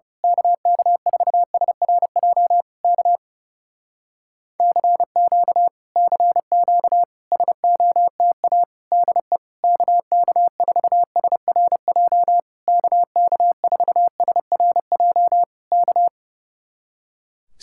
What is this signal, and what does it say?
Morse code, heard as a keyed tone, about 28 wpm, 700 Hz. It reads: KK4SRJ K CQ CQ SOTA DE KK4SRJ KK4SRJ K
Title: Speed-Racing Common QSO Phrases - 30wpm